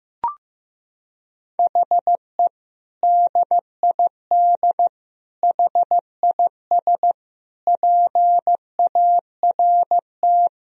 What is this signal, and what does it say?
Morse code, heard as a keyed tone, about 15 wpm, 700 Hz. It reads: HE DID HIS PART
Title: Sentences from Top 1000 Words - Review Part 1 - 15wpm